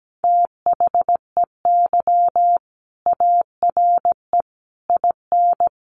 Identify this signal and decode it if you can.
Morse code, heard as a keyed tone, about 17 wpm, 700 Hz. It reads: THEY ARE IN
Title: Sentences from Top 1000 Words 17wpm